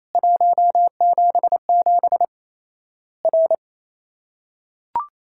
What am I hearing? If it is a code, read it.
Morse code, heard as a keyed tone, about 28 wpm, 700 Hz. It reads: 177 R
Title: Worked All Europe DX Contest - QTC Exchange - 28wpm